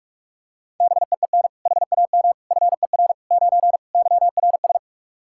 Morse code, heard as a keyed tone, about 46 wpm, 700 Hz. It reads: BEEN HAM FER 9 YRS
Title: Practice QSO - Exchange 3 - More Info - 45wpm